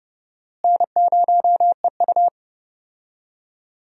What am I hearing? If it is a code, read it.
Morse code, heard as a keyed tone, about 30 wpm, 700 Hz. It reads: N0EU
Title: USA Call Signs - 4 Characters - Repeated 3x - 30wpm